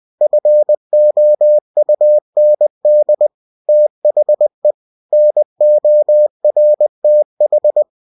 Morse code, heard as a keyed tone, about 20 wpm, 600 Hz. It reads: FOUND THE NORTH
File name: cw-books / winnie-the-pooh / winnie-20wpm-page-0087